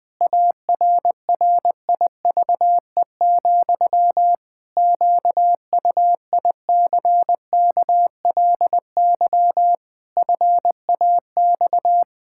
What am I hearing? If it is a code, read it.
Morse code, heard as a keyed tone, about 20 wpm, 700 Hz. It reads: ARRIVE, QUICKLY FAX
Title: Pangram Challenge v2 - 20wpm